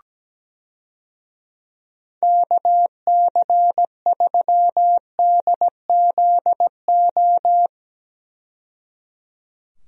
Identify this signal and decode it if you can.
Morse code, heard as a keyed tone, about 17 wpm, 700 Hz. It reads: KC3DZO
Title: USA Call Signs - 6 Characters - 17wpm